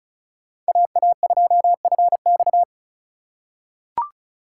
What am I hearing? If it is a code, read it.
Morse code, heard as a keyed tone, about 36 wpm, 700 Hz. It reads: AA2FX
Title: USA Call Signs - 5 Characters - 6x Character Spacing - 35wpm